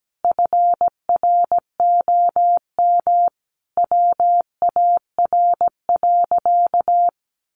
Morse code, heard as a keyed tone, about 17 wpm, 700 Hz. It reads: FROM WAR.
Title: Sentences from Top 1500 Words 17wpm